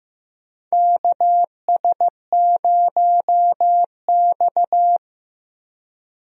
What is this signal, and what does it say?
Morse code, heard as a keyed tone, about 15 wpm, 700 Hz. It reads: KS0X
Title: USA Call Signs - 4 Characters - Repeated 2x - 15wpm